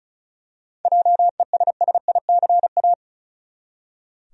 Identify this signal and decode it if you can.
Morse code, heard as a keyed tone, about 34 wpm, 700 Hz. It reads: JESSICA
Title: USA Common Names - 35wpm